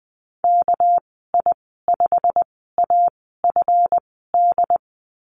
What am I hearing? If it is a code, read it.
Morse code, heard as a keyed tone, about 20 wpm, spaced as for 15 wpm, 700 Hz. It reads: KI5AFD
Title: 200 Lesson 32g - Letter k - Call Signs 20wpm